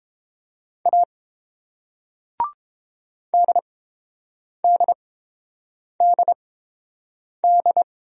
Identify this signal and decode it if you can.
Morse code, heard as a keyed tone, about 31 wpm, spaced as for 7 wpm, 700 Hz. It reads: A DDDD